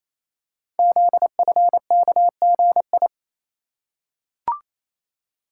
Morse code, heard as a keyed tone, about 28 wpm, 700 Hz. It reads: ZFKGI
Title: Code Groups - 5 Letters - 28wpm